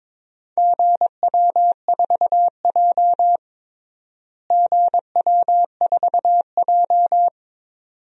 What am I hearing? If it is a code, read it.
Morse code, heard as a keyed tone, about 22 wpm, 700 Hz. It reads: GW4J GW4J
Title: DXCC-100 Call Signs - 4 Characters - Repeated 2x - 22wpm